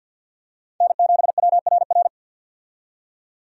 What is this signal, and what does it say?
Morse code, heard as a keyed tone, about 46 wpm, 700 Hz. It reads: N7WRR
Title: USA Call Signs - 5 Characters - 50wpm